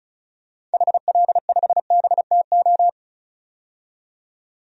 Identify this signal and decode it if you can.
Morse code, heard as a keyed tone, about 34 wpm, 700 Hz. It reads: HL5BTO
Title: DXCC-100 Call Signs - 6 Characters - 35wpm